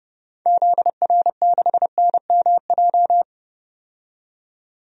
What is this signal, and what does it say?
Morse code, heard as a keyed tone, about 30 wpm, 700 Hz. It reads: ZR6NMJ
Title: DXCC-100 Call Signs - 6 Characters - 30wpm